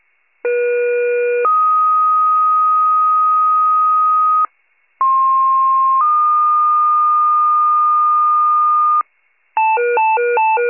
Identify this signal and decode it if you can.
Two-tone page, heard as a one-second tone, then a three-second tone, about 480 then 1250 Hz, about 1060 then 1220 Hz. It is heard from 0.4 to 9.0 s.
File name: wcalert